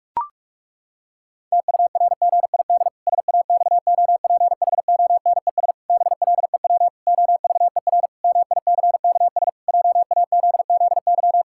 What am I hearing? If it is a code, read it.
Morse code, heard as a keyed tone, about 46 wpm, 700 Hz. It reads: TURGID SAXOPHONES BLEW OVER MICKS JAZZY
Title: Pangram Challenge v2 - 45wpm